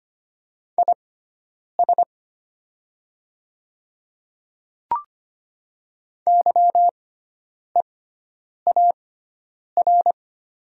Morse code, heard as a keyed tone, about 25 wpm, spaced as for 9 wpm, 700 Hz. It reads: IS YEAR
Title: 6x Character Spacing - Top 300 Words - 25wpm